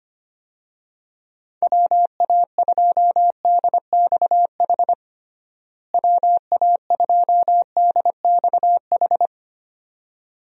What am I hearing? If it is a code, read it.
Morse code, heard as a keyed tone, about 25 wpm, 700 Hz. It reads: WA2DXH WA2DXH
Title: USA Call Signs - 6 Characters - Repeated 2x - 25wpm